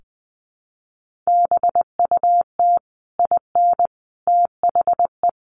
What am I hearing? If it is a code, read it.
Morse code, heard as a keyed tone, about 20 wpm, 700 Hz. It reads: BUT IN THE
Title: Sentences from Top 1500 Words 20wpm